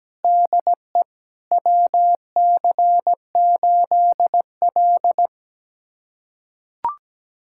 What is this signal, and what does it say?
Morse code, heard as a keyed tone, about 17 wpm, 700 Hz. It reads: DE WC8L
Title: USA Callsign DE Callsign - 4 Characters - 17wpm